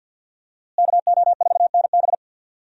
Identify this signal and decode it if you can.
Morse code, heard as a keyed tone, about 52 wpm, 700 Hz. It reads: KO4NB
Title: USA Call Signs - 5 Characters - Repeated 3x - 50wpm